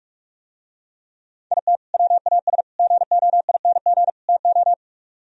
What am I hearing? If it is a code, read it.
Morse code, heard as a keyed tone, about 47 wpm, 700 Hz. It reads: IT WAS GOING TO